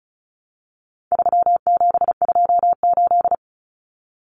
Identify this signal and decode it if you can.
Morse code, heard as a keyed tone, about 34 wpm, 700 Hz. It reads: 3728
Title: Common Cut Numbers - 4-Digit 35wpm